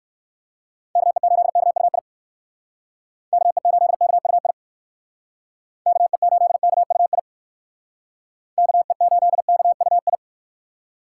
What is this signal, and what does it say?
Morse code, heard as a keyed tone, about 56 wpm, 700 Hz. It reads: KE8KUI KE8KUI KE8KUI KE8KUI
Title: Speed Racing - USA 2x3 Call Signs 45wpm